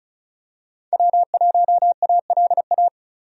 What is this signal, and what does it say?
Morse code, heard as a keyed tone, about 36 wpm, 700 Hz. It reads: W1ALA
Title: USA Call Signs - 5 Characters - Repeated 3x - 35wpm